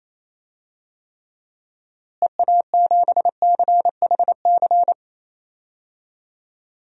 Morse code, heard as a keyed tone, about 28 wpm, 700 Hz. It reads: EA7CHC
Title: DXCC-100 Call Signs - 6 Characters - Repeated 3x - 28wpm